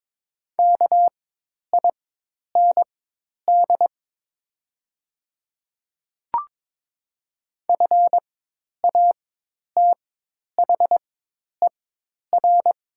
Morse code, heard as a keyed tone, about 22 wpm, spaced as for 10 wpm, 690 Hz. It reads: KIND FATHER
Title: 4x Character Spacing - Top 200 Words - 22wpm